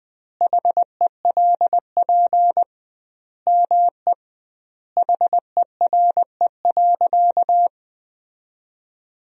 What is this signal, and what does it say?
Morse code, heard as a keyed tone, about 20 wpm, 700 Hz. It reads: HELP ME HERE.
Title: 2x Word-Spacing - Sentences from Top 200 Words - 20wpm